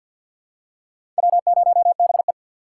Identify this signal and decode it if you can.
Morse code, heard as a keyed tone, about 50 wpm, 700 Hz. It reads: W0BE
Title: USA Call Signs - 4x Character Spacing - 50wpm